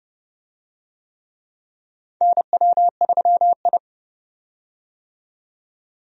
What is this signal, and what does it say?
Morse code, heard as a keyed tone, about 32 wpm, 700 Hz. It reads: NW3I
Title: USA Call Signs - 4 Characters - 30wpm